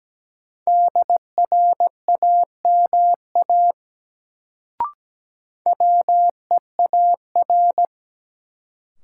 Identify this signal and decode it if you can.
Morse code, heard as a keyed tone, about 17 wpm, 700 Hz. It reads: DRAMA WEAR
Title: Sentences from Top 1500 Words - New Word Drill Alt - 17wpm